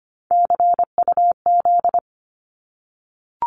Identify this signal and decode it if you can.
Morse code, heard as a keyed tone, about 25 wpm, 700 Hz. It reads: CUZ
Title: Abbreviations 25wpm